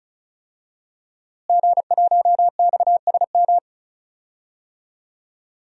Morse code, heard as a keyed tone, about 34 wpm, 700 Hz. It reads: G1XSM
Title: DXCC-100 Call Signs - 5 Characters - Repeated 2x - 35wpm